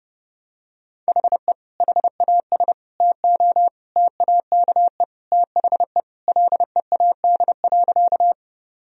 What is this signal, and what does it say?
Morse code, heard as a keyed tone, about 30 wpm, 700 Hz. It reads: HE HAS TO TAKE THE LEAD.
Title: Sentences from Top 1000 Words - Review Part 2 - 30wpm